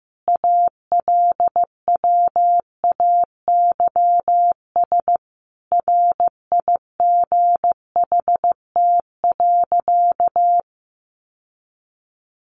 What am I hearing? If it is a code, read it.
Morse code, heard as a keyed tone, about 15 wpm, 700 Hz. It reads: ALWAYS RIGHT.
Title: Sentences from Top 1100 Words 15wpm